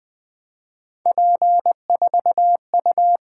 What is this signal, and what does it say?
Morse code, heard as a keyed tone, about 20 wpm, 700 Hz. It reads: P4U
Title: Three Letter-Number Combinations - 20wpm